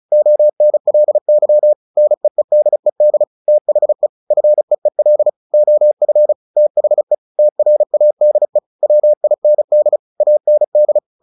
Morse code, heard as a keyed tone, about 34 wpm, 600 Hz. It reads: ONLY NEEDED THE FEEL OF THE TRADE WIND AND
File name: the old man and the sea 2 35wpm